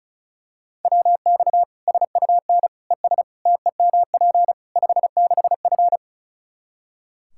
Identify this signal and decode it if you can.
Morse code, heard as a keyed tone, about 36 wpm, 700 Hz. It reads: WX SUN ES TEMP 56F
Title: Practice QSO - Exchange 2 - Info - 35wpm